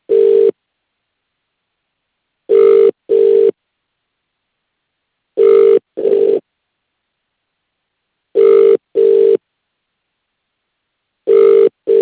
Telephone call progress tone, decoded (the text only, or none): ringback tone